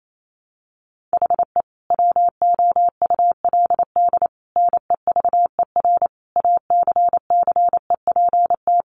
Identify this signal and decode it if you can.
Morse code, heard as a keyed tone, about 28 wpm, 700 Hz. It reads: HE WOULD NEVER ACCEPT
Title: Sentences from Top 1300 Words 28wpm